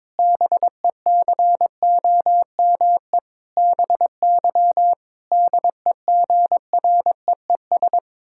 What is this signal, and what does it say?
Morse code, heard as a keyed tone, about 22 wpm, 700 Hz. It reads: BECOME BY DEGREES
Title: Pangram Challenge v2 - 22wpm